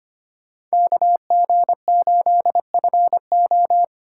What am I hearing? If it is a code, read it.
Morse code, heard as a keyed tone, about 25 wpm, 700 Hz. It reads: KG8FO
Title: USA Call Signs - 5 Characters - 2x Character Spacing - 25wpm